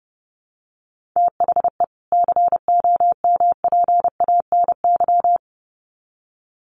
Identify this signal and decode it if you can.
Morse code, heard as a keyed tone, about 30 wpm, 700 Hz. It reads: THE COMPANY
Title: Sets of 2 Words 30wpm